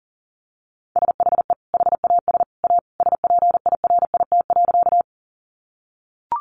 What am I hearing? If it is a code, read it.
Morse code, heard as a keyed tone, about 40 wpm, 700 Hz. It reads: SHE HAS A SPIRIT.
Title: Sentences from Top 1400 Words 40wpm